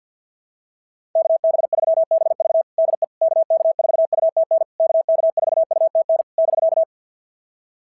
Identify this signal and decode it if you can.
Morse code, heard as a keyed tone, about 50 wpm, 650 Hz. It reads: KB3BV DE KK4UTN KK4UTN <BK>
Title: SOTA QSO Practice 50wpm